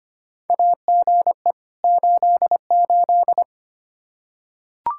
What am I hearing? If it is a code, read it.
Morse code, heard as a keyed tone, about 25 wpm, 700 Hz. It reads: AGE 88
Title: Practice QSO - Exchange 3 - More Info - 25wpm